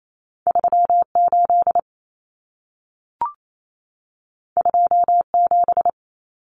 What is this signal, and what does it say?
Morse code, heard as a keyed tone, about 28 wpm, 700 Hz. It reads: 38 27
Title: Numbers - 2-Digit 28wpm